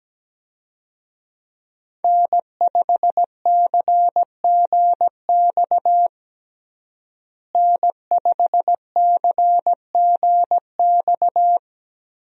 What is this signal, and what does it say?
Morse code, heard as a keyed tone, about 17 wpm, 700 Hz. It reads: N5CGX N5CGX
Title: USA Call Signs - 5 Characters - Repeated 3x - 17wpm